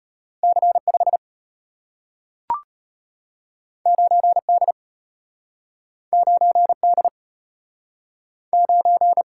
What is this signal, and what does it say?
Morse code, heard as a keyed tone, about 34 wpm, 700 Hz. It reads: C5 9D 9D 9